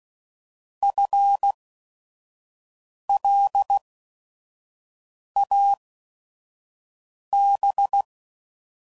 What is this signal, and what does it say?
Morse code, heard as a keyed tone, about 16 wpm, spaced as for 5 wpm, 800 Hz. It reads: FLAB
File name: Prefixes / By First Letter / Fs